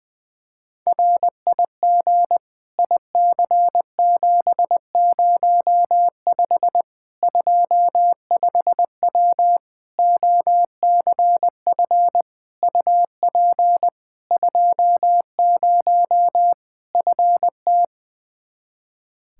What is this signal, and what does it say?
Morse code, heard as a keyed tone, about 21 wpm, 700 Hz. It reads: RIG IC705 25W OCF UP 20 FT
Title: K1USN SST QSO Practice 20wpm